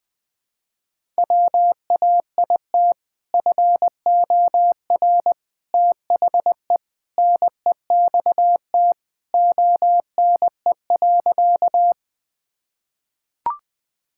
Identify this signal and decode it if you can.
Morse code, heard as a keyed tone, about 20 wpm, 700 Hz. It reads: WAIT FOR THE NEXT ONE.
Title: Sentences with 2-Word Phrases - 20wpm